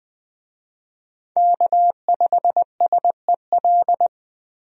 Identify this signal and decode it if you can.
Morse code, heard as a keyed tone, about 20 wpm, 700 Hz. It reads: K5SEL
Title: USA Call Signs - 5 Characters - 20wpm